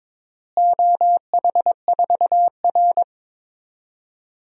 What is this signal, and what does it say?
Morse code, heard as a keyed tone, about 22 wpm, 700 Hz. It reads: OH4R
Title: DXCC-100 Call Signs - 4 Characters - Repeated 3x - 22wpm